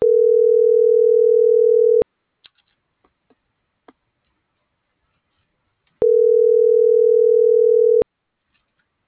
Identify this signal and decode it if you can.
Ringback tone (North America), 440 + 480 Hz.